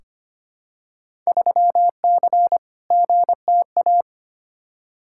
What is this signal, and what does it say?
Morse code, heard as a keyed tone, about 25 wpm, 700 Hz. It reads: 3C GTA
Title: Field Day Exchanges Encore - 25wpm